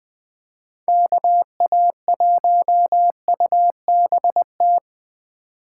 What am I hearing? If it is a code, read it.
Morse code, heard as a keyed tone, about 20 wpm, 700 Hz. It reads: KA1UBT
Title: Speed Racing - USA 2x3 Call Signs 17wpm